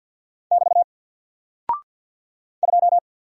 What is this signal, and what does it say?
Morse code, heard as a keyed tone, about 48 wpm, 700 Hz. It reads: <BT> 2
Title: Single Character (Letter:Number:Punctuation:Prosign) Alt 50wpm